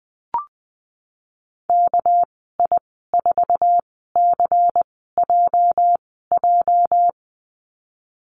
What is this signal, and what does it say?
Morse code, heard as a keyed tone, about 19 wpm, spaced as for 14 wpm, 700 Hz. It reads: KI4CJJ
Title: 219 Lesson 35g - Number 0 - Call Signs 20wpm